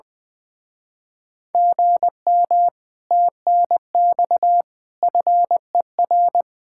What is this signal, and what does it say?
Morse code, heard as a keyed tone, about 20 wpm, 700 Hz. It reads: GM TNX FER
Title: Speed-Racing Common QSO Phrases - 20wpm